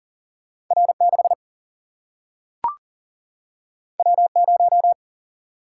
Morse code, heard as a keyed tone, about 39 wpm, 700 Hz. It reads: R6 W0
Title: Single-Letter Single-Number 40wpm